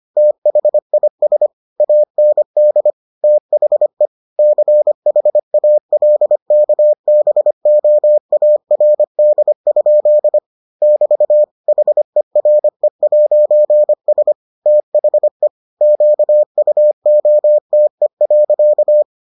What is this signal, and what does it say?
Morse code, heard as a keyed tone, about 25 wpm, 600 Hz. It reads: THIS AND THE CHALKBOARD? <BT> HERE'S THE QUOTE.